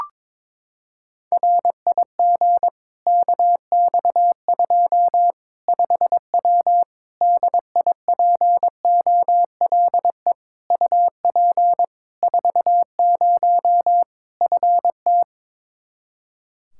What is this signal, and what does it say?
Morse code, heard as a keyed tone, about 22 wpm, 700 Hz. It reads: RIG KX2 5W DIPOLE UP 40 FT
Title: Practice QSO - Exchange 2 - Info - 22wpm